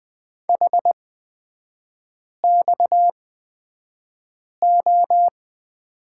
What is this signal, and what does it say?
Morse code, heard as a keyed tone, about 20 wpm, spaced as for 5 wpm, 690 Hz. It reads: HXO
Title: 229 Lesson 37e - Letter x - Three Characters 20wpm